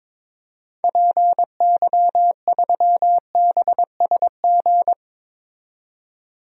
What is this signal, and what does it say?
Morse code, heard as a keyed tone, about 22 wpm, 700 Hz. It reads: PY3BSG
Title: DXCC-100 Call Signs - 6 Characters - 22wpm